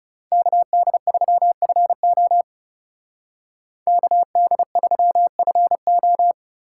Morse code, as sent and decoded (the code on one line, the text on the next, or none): -.- -.. ...-- ..-. --- / -.- -.. ...-- ..-. ---
KD3FO KD3FO